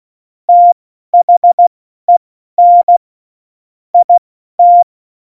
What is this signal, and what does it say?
Morse code, heard as a keyed tone, about 16 wpm, spaced as for 12 wpm, 700 Hz. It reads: THEN IT
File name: Morsekurs / Witze